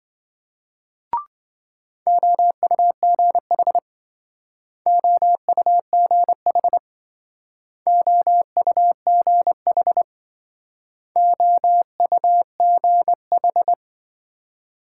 Morse code, heard as a keyed tone, about 26 wpm, 700 Hz. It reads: OUGH OUGH OUGH OUGH